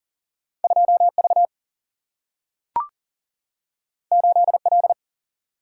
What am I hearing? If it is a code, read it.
Morse code, heard as a keyed tone, about 40 wpm, 700 Hz. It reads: 2V 8L
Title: Single-Letter Single-Number 40wpm